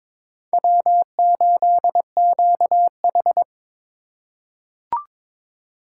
Morse code, heard as a keyed tone, about 22 wpm, 700 Hz. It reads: W8QH
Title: USA Call Signs - 4 Characters - 4x Character Spacing - 22wpm